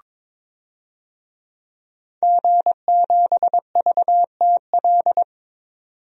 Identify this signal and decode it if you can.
Morse code, heard as a keyed tone, about 22 wpm, 700 Hz. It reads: G7VTL
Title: DXCC-100 Call Signs - 5 Characters - 22wpm